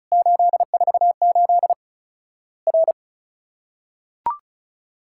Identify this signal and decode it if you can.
Morse code, heard as a keyed tone, about 36 wpm, 700 Hz. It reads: 848 R
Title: Worked All Europe DX Contest - QTC Exchange - 35wpm